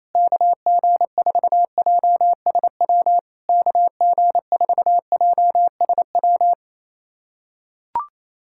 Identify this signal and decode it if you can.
Morse code, heard as a keyed tone, about 28 wpm, 700 Hz. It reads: KG4JSW KG4JSW
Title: SOTA QSO Practice 28wpm